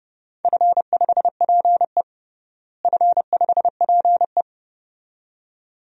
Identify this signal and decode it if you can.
Morse code, heard as a keyed tone, about 30 wpm, 700 Hz. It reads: F5PE F5PE
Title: DXCC-100 Call Signs - 4 Characters - Repeated 2x - 30wpm